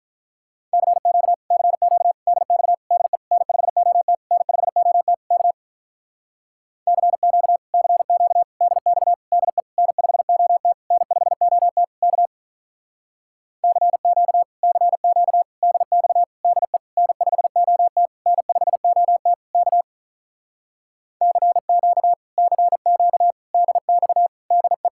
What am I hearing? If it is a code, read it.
Morse code, heard as a keyed tone, about 48 wpm, 700 Hz. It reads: CQ CQ DX DE N5OT N5OT K CQ CQ DX DE N5OT N5OT K CQ CQ DX DE N5OT N5OT K CQ CQ DX DE